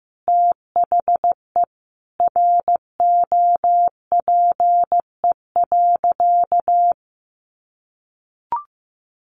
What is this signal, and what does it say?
Morse code, heard as a keyed tone, about 15 wpm, 700 Hz. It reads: THE ROPE.
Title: Easy Phrases 15wpm